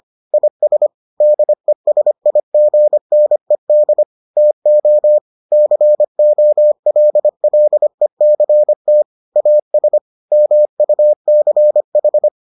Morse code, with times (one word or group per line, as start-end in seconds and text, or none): 0.3-0.9 IS
1.2-4.0 DESIGNED
4.4-5.2 TO
5.5-9.0 COLLECT
9.4-10.0 AS
10.3-12.3 MUCH